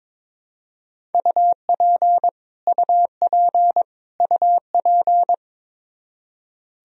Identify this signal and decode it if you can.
Morse code, heard as a keyed tone, about 22 wpm, 700 Hz. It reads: UP UP UP
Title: Top 100 Words Repeated 22wpm